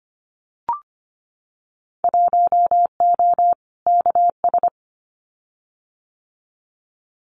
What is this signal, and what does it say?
Morse code, heard as a keyed tone, about 25 wpm, 700 Hz. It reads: 1O KS
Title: Winter Field Day Exchanges 25wpm